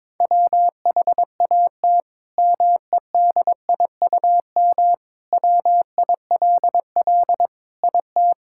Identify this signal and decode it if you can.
Morse code, heard as a keyed tone, about 22 wpm, 700 Hz. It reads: WHAT MEDIUM WILL IT
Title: Sentences from Top 1000 Words - Review Part 6 - 22wpm